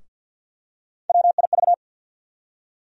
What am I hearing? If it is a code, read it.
Morse code, heard as a keyed tone, about 50 wpm, 700 Hz. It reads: WIV
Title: USA Call Sign Postfixes - 3 Letters 50wpm